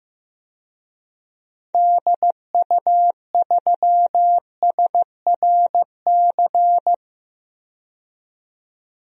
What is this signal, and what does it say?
Morse code, heard as a keyed tone, about 16 wpm, 700 Hz. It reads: DU3SRC
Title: DXCC-100 Call Signs - 6 Characters - 15wpm